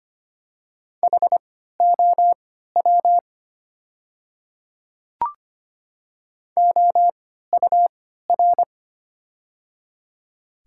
Morse code, heard as a keyed tone, about 25 wpm, spaced as for 14 wpm, 700 Hz. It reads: HOW OUR
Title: 3x Character Spacing - Top 200 Words - 25wpm